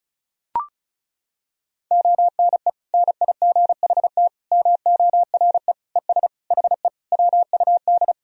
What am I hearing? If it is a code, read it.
Morse code, heard as a keyed tone, about 36 wpm, 700 Hz. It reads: ONE NIGHT MORE ES HE WUD
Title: Sentences from Top 300 Words and Abbreviations 35wpm